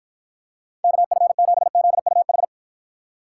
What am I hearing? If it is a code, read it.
Morse code, heard as a keyed tone, about 53 wpm, 700 Hz. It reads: KF7ZUH